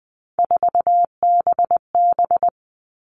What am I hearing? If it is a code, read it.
Morse code, heard as a keyed tone, about 20 wpm, 700 Hz. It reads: 4BB
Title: All Cut Numbers - 3-Digit 20wpm